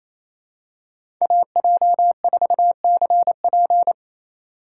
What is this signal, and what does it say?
Morse code, heard as a keyed tone, about 28 wpm, 700 Hz. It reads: AJ4CP
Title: USA Call Signs - 5 Characters - 2x Character Spacing - 28wpm